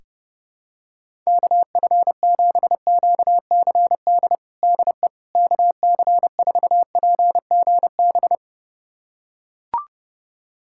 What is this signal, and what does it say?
Morse code, heard as a keyed tone, about 32 wpm, 700 Hz. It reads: KF7QCD DE KC4PGB
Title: USA Callsign DE Callsign 30wpm